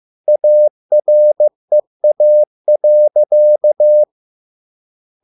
Morse code, heard as a keyed tone, about 15 wpm, 600 Hz. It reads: AREA.